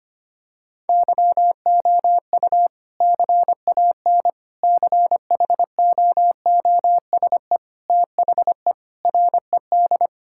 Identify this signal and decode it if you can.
Morse code, heard as a keyed tone, about 25 wpm, 700 Hz. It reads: YOU CAN CHOOSE THE RED
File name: practice / 25wpm / Sentences with 2-Word Phrases - 25wpm